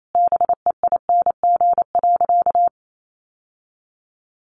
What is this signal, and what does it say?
Morse code, heard as a keyed tone, about 28 wpm, 700 Hz. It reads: BEING.
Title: Sentences from Top 1000 Words 28wpm